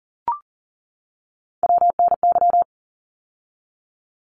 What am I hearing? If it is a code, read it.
Morse code, heard as a keyed tone, about 40 wpm, 700 Hz. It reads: WNY
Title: ARRL Sections 40wpm